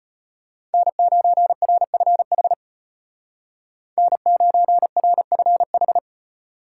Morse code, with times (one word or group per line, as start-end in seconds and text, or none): 0.7-2.5 N9RFH
4.0-6.0 N9RFH